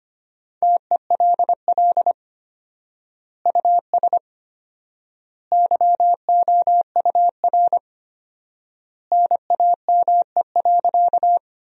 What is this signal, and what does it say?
Morse code, heard as a keyed tone, about 25 wpm, 700 Hz. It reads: TELL US YOUR NAME.
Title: 4x Word-Spacing - Sentences from Top 200 Words - 25wpm